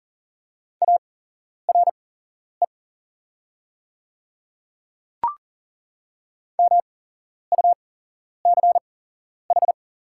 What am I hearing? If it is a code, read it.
Morse code, heard as a keyed tone, about 40 wpm, spaced as for 11 wpm, 700 Hz. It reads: ARE MUCH
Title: 8x Character Spacing - Top 200 Words - 40wpm